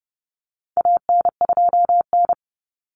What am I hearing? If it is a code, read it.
Morse code, heard as a keyed tone, about 30 wpm, 700 Hz. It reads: AN2N
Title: Common Cut Numbers - 4-Digit 30wpm